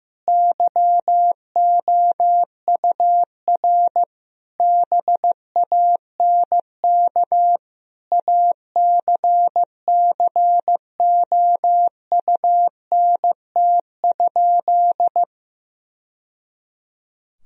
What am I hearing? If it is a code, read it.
Morse code, heard as a keyed tone, about 15 wpm, 700 Hz. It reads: YOUR BANK ACCOUNT?
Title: Sentences from Top 1000 Words - Review Part 4 - 15wpm